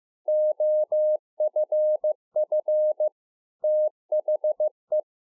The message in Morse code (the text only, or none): OFF THE